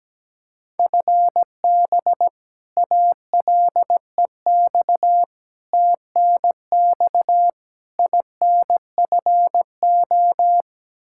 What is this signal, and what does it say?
Morse code, heard as a keyed tone, about 17 wpm, 700 Hz. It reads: FB ALEX TNX INFO
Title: Practice QSO - Exchange 3 - More Info - 17wpm